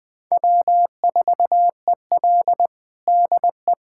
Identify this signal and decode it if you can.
Morse code, heard as a keyed tone, about 21 wpm, 700 Hz. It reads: W4EL DE
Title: USA Callsign DE Callsign - 4 Characters - 20wpm